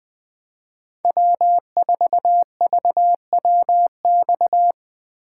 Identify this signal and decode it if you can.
Morse code, heard as a keyed tone, about 22 wpm, 700 Hz. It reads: W4VWX
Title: Speed Racing - USA 1x3 Call Signs 20wpm